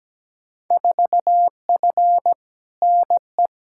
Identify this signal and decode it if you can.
Morse code, heard as a keyed tone, about 17 wpm, 700 Hz. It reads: 4F NE
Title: Field Day Exchanges Encore - 17wpm